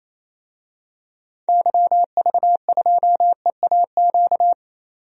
Morse code, heard as a keyed tone, about 28 wpm, 700 Hz. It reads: YV2EAQ
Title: DXCC-100 Call Signs - 6 Characters - 28wpm